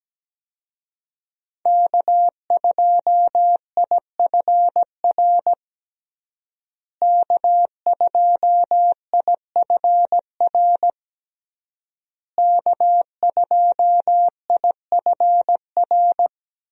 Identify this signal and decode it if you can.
Morse code, heard as a keyed tone, about 17 wpm, 700 Hz. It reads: K2IFR K2IFR K2IFR
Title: USA Call Signs - 5 Characters - Repeated 3x - 17wpm